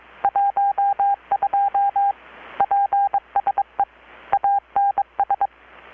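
Morse code, heard as a keyed tone, about 22 wpm, 800 Hz. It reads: 12 PSE ANS
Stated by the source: Morse from KPH's Night of Nights XX maritime radiotelegraphy event, received in Chiapas, Mexico about 3,800 km away on a Kenwood R-600 with a grounded 10 m long-wire antenna.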